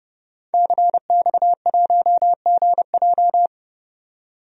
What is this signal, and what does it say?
Morse code, heard as a keyed tone, about 30 wpm, 700 Hz. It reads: CX1GJ
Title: DXCC-100 Call Signs - 5 Characters - 30wpm